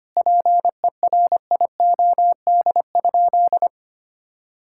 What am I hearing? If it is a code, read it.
Morse code, heard as a keyed tone, about 25 wpm, 700 Hz. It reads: PERIOD?
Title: Sentences from Top 1000 Words - Review Part 5 - 25wpm